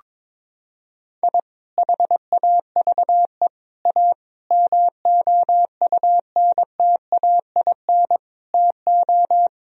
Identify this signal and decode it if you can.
Morse code, heard as a keyed tone, about 22 wpm, 700 Hz. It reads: I HAVE A MOUNTAIN TO
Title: Sentences from Top 1000 Words - Review Part 1 - 22wpm